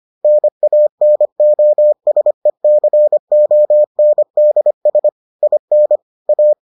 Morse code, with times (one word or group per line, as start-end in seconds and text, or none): 0.2-5.1 NANOSECONDS
5.4-6.0 IN
6.3-6.5 A